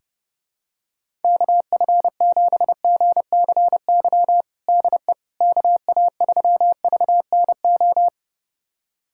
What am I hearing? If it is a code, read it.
Morse code, heard as a keyed tone, about 32 wpm, 700 Hz. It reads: KF7GCY DE KA3VNO
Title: USA Callsign DE Callsign - 6 Characters - 30wpm